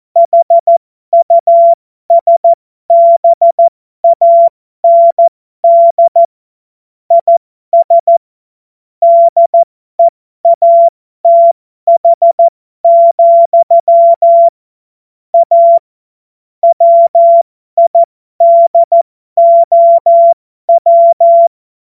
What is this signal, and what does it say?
Morse code, heard as a keyed tone, about 14 wpm, spaced as for 12 wpm, 680 Hz. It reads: HUSBAND IS DEATH, A WIDOW
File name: Morsekurs / Witze